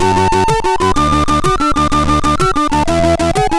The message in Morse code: none